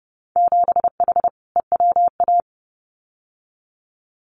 Morse code, heard as a keyed tone, about 30 wpm, 700 Hz. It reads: 7H EWA
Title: Winter Field Day Exchanges Encore 30wpm